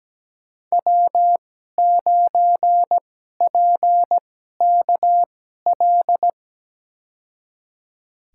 Morse code, heard as a keyed tone, about 17 wpm, spaced as for 12 wpm, 700 Hz. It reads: W9PKL
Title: USA Call Signs - 5 Characters - 2x Character Spacing - 17wpm